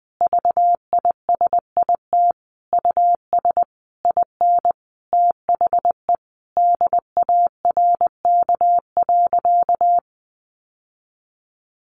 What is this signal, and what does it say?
Morse code, heard as a keyed tone, about 20 wpm, 700 Hz. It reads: VISIT US IN THE DARK.
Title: Sentences from Top 1400 Words 20wpm